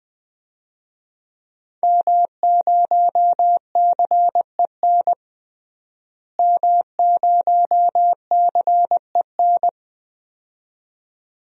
Morse code, heard as a keyed tone, about 20 wpm, 700 Hz. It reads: M0CEN M0CEN
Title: DXCC-100 Call Signs - 5 Characters - Repeated 2x - 20wpm